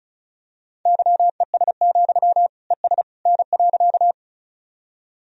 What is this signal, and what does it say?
Morse code, heard as a keyed tone, about 34 wpm, 700 Hz. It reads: YES, ES N.
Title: Sentences from Top 500 Words and Abbreviations 35wpm